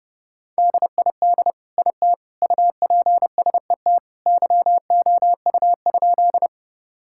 Morse code, heard as a keyed tone, about 32 wpm, 700 Hz. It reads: DID IT UPSET YOU?
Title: Sentences from Top 1000 Words - Review Part 5 - 30wpm